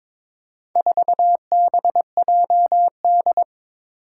Morse code, heard as a keyed tone, about 22 wpm, 700 Hz. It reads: 4BJD
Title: Four Challenging Letter-Number Combinations 22wpm